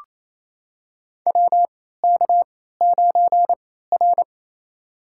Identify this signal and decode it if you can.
Morse code, heard as a keyed tone, about 28 wpm, spaced as for 16 wpm, 700 Hz. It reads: WK9R
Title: USA Call Signs - 3x Character Spacing - 28wpm